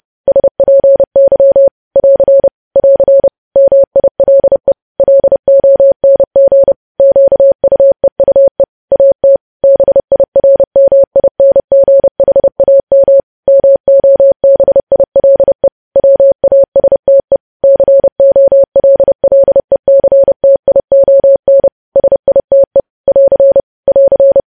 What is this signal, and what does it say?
Morse code, heard as a keyed tone, about 30 wpm, 550 Hz. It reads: SPY <AR> <AR> MILE LONG QUEUE AT BIRMINGHAM MOBILE WASTE COLLECTION SITE <AR> <AR>
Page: news headlines in morse code at 30 wpm